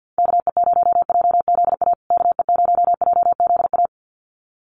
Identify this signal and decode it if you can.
Morse code, heard as a keyed tone, about 50 wpm, 700 Hz. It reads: KE0JZA KE0JZA
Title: USA Call Signs Repeated 2x 50wpm